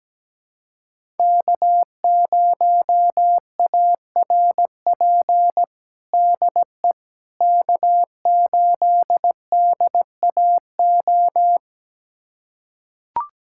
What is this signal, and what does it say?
Morse code, heard as a keyed tone, about 17 wpm, 700 Hz. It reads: K0ARP DE K8DAO
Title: USA Callsign DE Callsign - 5 Characters - 17wpm